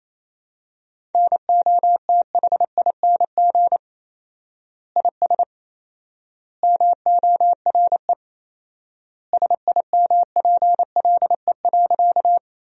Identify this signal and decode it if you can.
Morse code, heard as a keyed tone, about 28 wpm, 700 Hz. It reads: NOTHING IS MORE SIMPLE.
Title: 4x Word-Spacing - Sentences from Top 500 Words - 28wpm